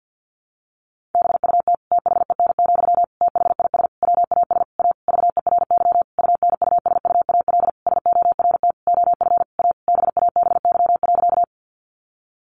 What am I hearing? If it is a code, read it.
Morse code, heard as a keyed tone, about 50 wpm, 700 Hz. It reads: BUT THEN, THIS WAS A VERY UNUSUAL SORT OF A BABY.
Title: Sentences from Top 1400 Words 50wpm